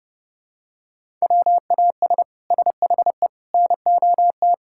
Morse code, heard as a keyed tone, about 32 wpm, 700 Hz. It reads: WAS SHE NOT
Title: Sentences from Top 1000 Words - Review Part 5 - 30wpm